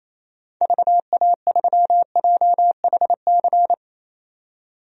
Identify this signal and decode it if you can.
Morse code, heard as a keyed tone, about 28 wpm, 700 Hz. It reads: VA3JHC
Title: DXCC-100 Call Signs - 6 Characters - 28wpm